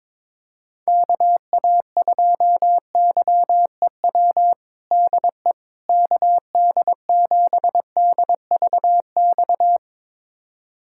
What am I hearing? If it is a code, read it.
Morse code, heard as a keyed tone, about 21 wpm, 700 Hz. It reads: KA2YEW DE KD7DVX
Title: USA Callsign DE Callsign - 6 Characters - 22wpm